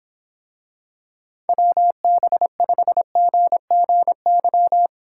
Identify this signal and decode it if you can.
Morse code, heard as a keyed tone, about 25 wpm, 700 Hz. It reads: WB5GGY